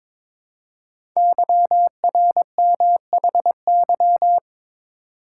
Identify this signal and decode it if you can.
Morse code, heard as a keyed tone, about 23 wpm, 700 Hz. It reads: YRMHY
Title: Code Groups - 5 Letters - 22wpm